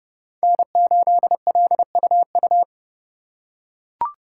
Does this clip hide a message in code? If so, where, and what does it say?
Yes: Morse code from 0.4 to 2.6 s, heard as a keyed tone, about 28 wpm, 700 Hz. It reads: N8LUU